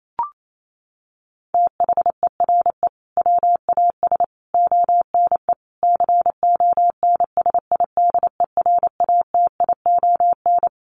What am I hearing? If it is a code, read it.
Morse code, heard as a keyed tone, about 28 wpm, 700 Hz. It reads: THERE WAS ONE CONSIDERATION
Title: Sentences from Top 1500 Words 28wpm